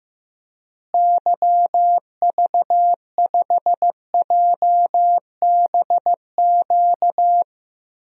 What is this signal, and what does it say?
Morse code, heard as a keyed tone, about 16 wpm, 700 Hz. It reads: YV5JBQ
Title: DXCC-100 Call Signs - 6 Characters - Repeated 2x - 15wpm